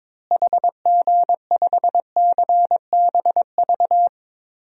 Morse code, heard as a keyed tone, about 21 wpm, 700 Hz. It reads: HG5CBV
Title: DXCC-100 Call Signs - 6 Characters - Repeated 2x - 22wpm